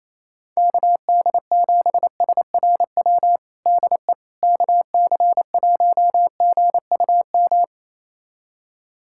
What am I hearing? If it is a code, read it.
Morse code, heard as a keyed tone, about 28 wpm, 700 Hz. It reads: KD7SRW DE KC1GUM
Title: USA Callsign DE Callsign - 6 Characters - 28wpm